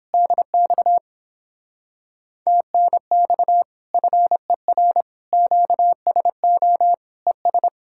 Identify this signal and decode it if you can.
Morse code, heard as a keyed tone, about 31 wpm, 700 Hz. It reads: DX TNX FER QSO ES